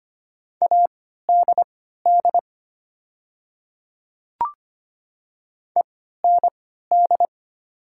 Morse code, heard as a keyed tone, about 25 wpm, spaced as for 14 wpm, 700 Hz. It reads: ADD END
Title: 3x Character Spacing - Top 200 Words - 25wpm